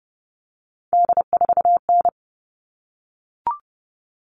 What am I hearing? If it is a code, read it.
Morse code, heard as a keyed tone, about 29 wpm, 700 Hz. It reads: D4N
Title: All Cut Numbers - 3-Digit 30wpm